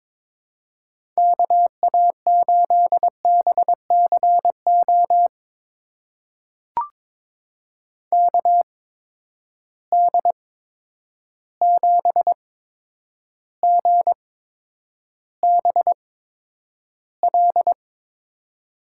Morse code, heard as a keyed tone, about 23 wpm, spaced as for 10 wpm, 700 Hz. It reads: KA8BCO KD7GBL
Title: USA Call Signs - 6 Characters - 8x Character Spacing - 22wpm